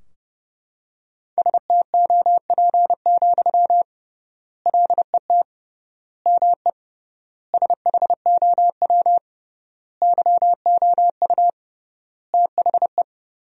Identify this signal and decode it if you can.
Morse code, heard as a keyed tone, about 30 wpm, 700 Hz. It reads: STOP, LET ME SHOW YOU THE
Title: 3x Word-Spacing - Sentences from Top 300 Words - 30wpm